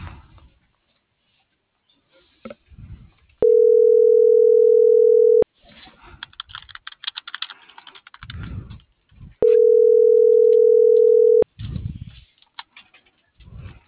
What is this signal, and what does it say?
Ringback tone (North America), 440 + 480 Hz.
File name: CENTRAL MULTIMODAL / out viridiana.juarez - .85409